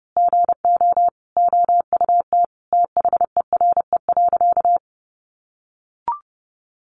Morse code, heard as a keyed tone, about 30 wpm, 700 Hz. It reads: GO OUT THERE.
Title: A Sound Of Thunder 30wpm